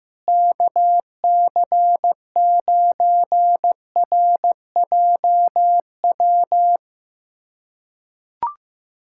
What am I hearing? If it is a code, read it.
Morse code, heard as a keyed tone, about 15 wpm, 700 Hz. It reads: KC9RJW
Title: USA Callsign DE Callsign - 6 Characters - 15wpm